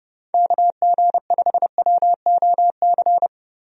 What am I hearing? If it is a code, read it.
Morse code, heard as a keyed tone, about 30 wpm, 700 Hz. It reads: KG5WOC
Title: USA Call Signs - 6 Characters - Repeated 3x - 30wpm